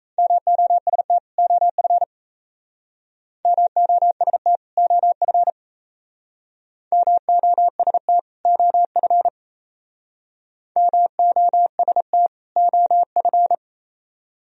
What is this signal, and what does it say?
Morse code, heard as a keyed tone, about 36 wpm, 700 Hz. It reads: MOST OF MOST OF MOST OF MOST OF